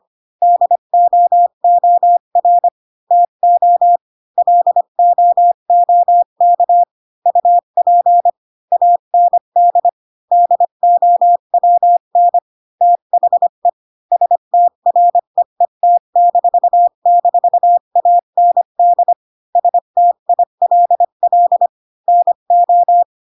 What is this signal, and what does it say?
Morse code, heard as a keyed tone, about 25 wpm, 700 Hz. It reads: DOOR TO LOOK UP AND DOWN THE STREET--AND STILL NO
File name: Pinocchio / Pinocchio ch1 20 25 ik1hge